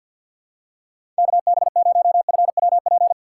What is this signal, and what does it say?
Morse code, heard as a keyed tone, about 53 wpm, 700 Hz. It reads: KB0FWP
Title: USA Call Signs - 6 Characters - 6x Character Spacing - 50wpm